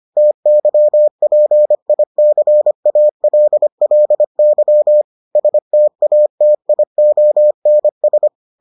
Morse code, heard as a keyed tone, about 25 wpm, 600 Hz. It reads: TYPICALLY STATIONS